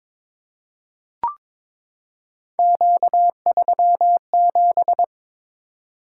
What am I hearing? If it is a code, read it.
Morse code, heard as a keyed tone, about 22 wpm, 700 Hz. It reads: Q37
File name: practice / 22wpm / Three Letter-Number Combinations - 22wpm